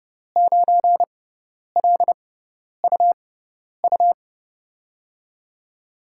Morse code, heard as a keyed tone, about 30 wpm, spaced as for 10 wpm, 700 Hz. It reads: 9LUU
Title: USA Call Signs - 5 Characters - 6x Character Spacing - 30wpm